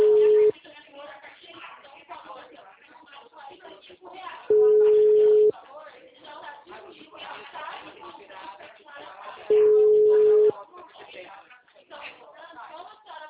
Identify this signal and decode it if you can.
Ringback tone (Europe, 425 Hz), with speech.